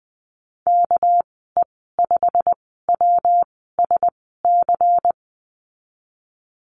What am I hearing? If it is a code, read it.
Morse code, heard as a keyed tone, about 20 wpm, spaced as for 14 wpm, 700 Hz. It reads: KE5WSC
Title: 231 Lesson 37g - Letter x - Call Signs 20wpm